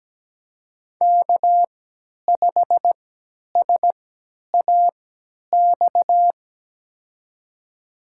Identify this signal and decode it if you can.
Morse code, heard as a keyed tone, about 17 wpm, spaced as for 10 wpm, 700 Hz. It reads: K5SAX
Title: USA Call Signs - 3x Character Spacing - 17wpm